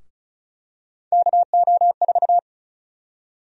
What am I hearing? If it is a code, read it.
Morse code, heard as a keyed tone, about 36 wpm, 700 Hz. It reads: KO4
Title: USA Call Sign Prefixes - 35wpm